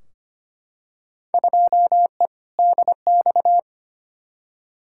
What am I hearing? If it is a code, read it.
Morse code, heard as a keyed tone, about 25 wpm, 700 Hz. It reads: 2E DX
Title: Field Day Exchanges - 25wpm